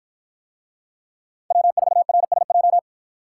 Morse code, heard as a keyed tone, about 50 wpm, 700 Hz. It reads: W4RSJ